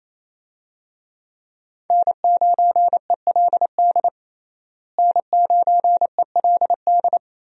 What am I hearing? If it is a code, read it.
Morse code, heard as a keyed tone, about 28 wpm, 700 Hz. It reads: N9ELD N9ELD
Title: USA Call Signs - 5 Characters - Repeated 2x - 28wpm